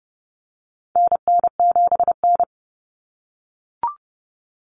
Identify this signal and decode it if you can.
Morse code, heard as a keyed tone, about 30 wpm, 700 Hz. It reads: NN7N
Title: Common Cut Numbers - 4-Digit 30wpm